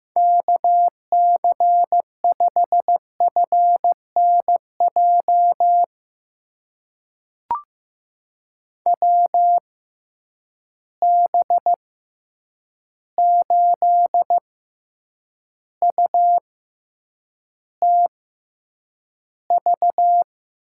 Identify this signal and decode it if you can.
Morse code, heard as a keyed tone, about 15 wpm, spaced as for 8 wpm, 700 Hz. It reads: KC5FNJ WB8UTV
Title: USA Call Signs - 6 Characters - 6x Character Spacing - 15wpm